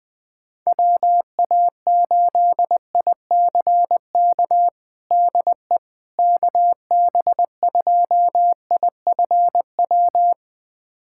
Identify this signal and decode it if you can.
Morse code, heard as a keyed tone, about 20 wpm, 700 Hz. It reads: WA8ICK DE KB2IFW
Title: USA Callsign DE Callsign 20wpm